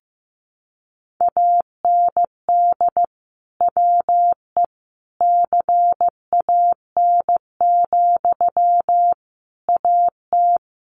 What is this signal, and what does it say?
Morse code, heard as a keyed tone, about 15 wpm, 700 Hz. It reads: AND WE CAN, AT
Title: Sentences from Top 900 Words 15wpm